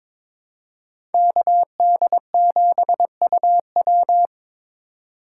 Morse code, heard as a keyed tone, about 23 wpm, 700 Hz. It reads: KD7UW
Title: USA Call Signs - 8x Character Spacing - 22wpm